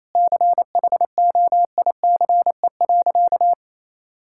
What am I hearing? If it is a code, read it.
Morse code, heard as a keyed tone, about 28 wpm, 700 Hz. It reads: CHOICE.
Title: Sentences from Top 1600 Words 28wpm